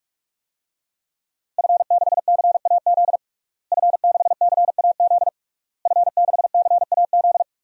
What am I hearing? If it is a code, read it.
Morse code, heard as a keyed tone, about 46 wpm, 700 Hz. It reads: F6CAZ F6CAZ F6CAZ
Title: DXCC-100 Call Signs - 5 Characters - Repeated 3x - 45wpm